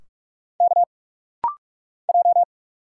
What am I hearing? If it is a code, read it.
Morse code, heard as a keyed tone, about 46 wpm, 700 Hz. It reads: K J
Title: Single Letters - Rapid-Fire 45wpm